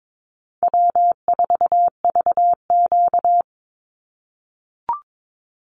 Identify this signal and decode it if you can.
Morse code, heard as a keyed tone, about 22 wpm, 700 Hz. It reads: W4VQ
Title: Easy Call Signs - 22wpm